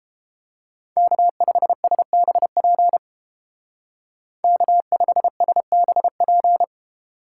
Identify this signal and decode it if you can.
Morse code, heard as a keyed tone, about 28 wpm, 700 Hz. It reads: K5SBP K5SBP